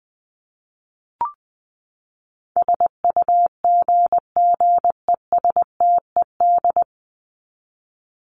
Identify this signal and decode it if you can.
Morse code, heard as a keyed tone, about 20 wpm, 700 Hz. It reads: SUGGESTED
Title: ED Suffix - Top 100 Words with Base Words 20wpm